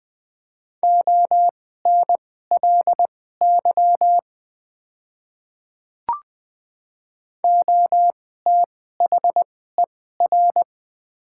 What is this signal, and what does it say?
Morse code, heard as a keyed tone, about 20 wpm, spaced as for 14 wpm, 700 Hz. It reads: ONLY OTHER
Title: 2x Character Spacing - Top 200 Words - 20wpm